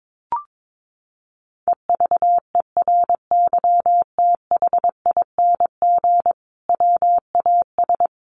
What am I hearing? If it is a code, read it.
Morse code, heard as a keyed tone, about 22 wpm, 700 Hz. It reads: EVERYTHING WAS
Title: Sentences from Top 900 Words 22wpm